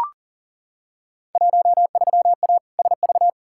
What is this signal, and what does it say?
Morse code, heard as a keyed tone, about 40 wpm, 700 Hz. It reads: 13A SV
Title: Field Day Exchanges Encore - 40wpm